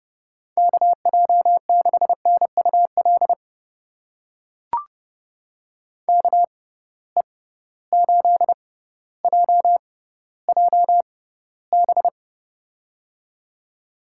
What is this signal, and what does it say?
Morse code, heard as a keyed tone, about 31 wpm, spaced as for 16 wpm, 700 Hz. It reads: KJ6NUL KE8JJB
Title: USA Call Signs - 6 Characters - 6x Character Spacing - 30wpm